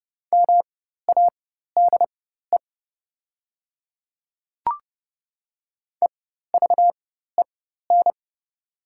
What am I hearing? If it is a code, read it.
Morse code, heard as a keyed tone, about 30 wpm, spaced as for 14 wpm, 700 Hz. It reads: MADE EVEN
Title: 4x Character Spacing - Top 300 Words - 30wpm